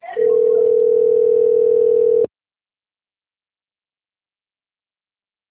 Ringback tone (North America), 440 + 480 Hz.